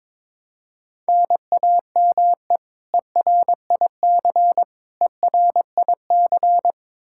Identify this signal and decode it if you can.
Morse code, heard as a keyed tone, about 21 wpm, 700 Hz. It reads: NAME ERIC ERIC
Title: Practice QSO - Exchange 1 - Beginning - 22wpm